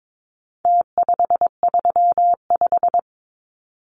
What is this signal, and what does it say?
Morse code, heard as a keyed tone, about 22 wpm, 700 Hz. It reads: T535
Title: Numbers - Contest Style - 4-Digit 22wpm